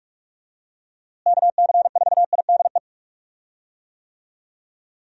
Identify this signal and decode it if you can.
Morse code, heard as a keyed tone, about 42 wpm, 700 Hz. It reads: KC4IDE
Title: USA Call Signs - 6 Characters - Repeated 2x - 45wpm